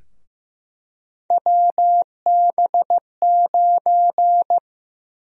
Morse code, heard as a keyed tone, about 15 wpm, 700 Hz. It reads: WB9
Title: USA Call Sign Prefixes - Repeated 2x - 15wpm